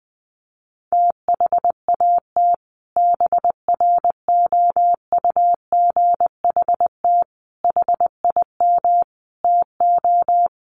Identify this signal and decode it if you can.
Morse code, heard as a keyed tone, about 20 wpm, 700 Hz. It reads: THAT BROUGHT HIM TO
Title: Sentences from Top 700 Words 20wpm